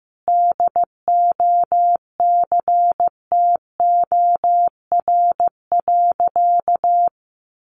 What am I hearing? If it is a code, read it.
Morse code, heard as a keyed tone, about 15 wpm, 700 Hz. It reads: DOCTOR.
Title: Sentences from Top 1500 Words 15wpm